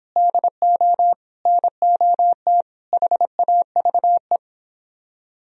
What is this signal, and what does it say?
Morse code, heard as a keyed tone, about 31 wpm, 700 Hz. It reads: DO NOT HAVE